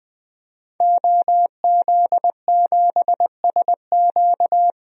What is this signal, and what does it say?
Morse code, heard as a keyed tone, about 20 wpm, 700 Hz. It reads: OZ7SQ
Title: DXCC-100 Call Signs - 5 Characters - 20wpm